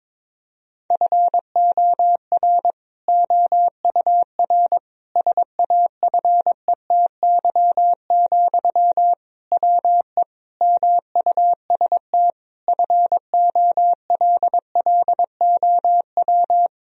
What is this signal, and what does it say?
Morse code, heard as a keyed tone, about 22 wpm, 700 Hz. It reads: FOR OUR SAFETY, WE MUST FOLLOW
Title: Sentences from Top 1000 Words - Review Part 4 - 22wpm